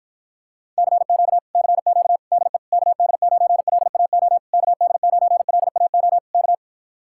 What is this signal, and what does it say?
Morse code, heard as a keyed tone, about 48 wpm, 700 Hz. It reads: CQ CQ DE KD9LAO KD9LAO K